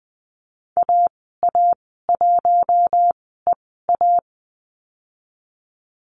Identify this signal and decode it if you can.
Morse code, heard as a keyed tone, about 20 wpm, spaced as for 14 wpm, 700 Hz. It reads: AA1EA
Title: 042 Lesson 8g - Number 1 - Call Signs 20wpm